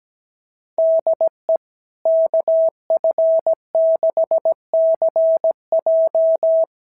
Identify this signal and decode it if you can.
Morse code, heard as a keyed tone, about 17 wpm, 650 Hz. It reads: DE KF6CJ
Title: SOTA QSO Practice 17wpm